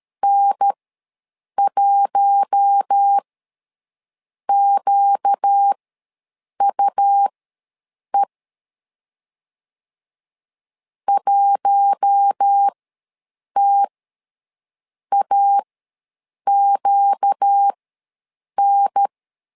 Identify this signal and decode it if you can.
Morse code, heard as a keyed tone, about 13 wpm, spaced as for 7 wpm, 800 Hz. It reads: N1QUE 1TAQN
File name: LEZIONI / 4 Q1 ws 12wpm 2129